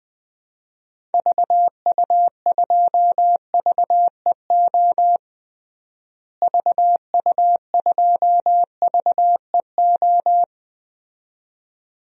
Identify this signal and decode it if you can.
Morse code, heard as a keyed tone, about 20 wpm, 700 Hz. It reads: VU2VEO VU2VEO
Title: DXCC-100 Call Signs - 6 Characters - Repeated 2x - 20wpm